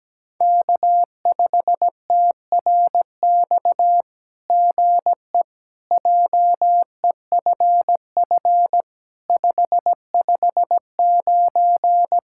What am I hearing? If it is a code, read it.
Morse code, heard as a keyed tone, about 17 wpm, 700 Hz. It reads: K5TRX GE JEFF 559
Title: SOTA QSO Practice 17wpm